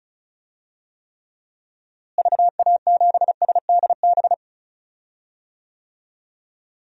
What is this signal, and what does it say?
Morse code, heard as a keyed tone, about 36 wpm, 700 Hz. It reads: VA7SDB
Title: DXCC-100 Call Signs - 6 Characters - Repeated 2x - 35wpm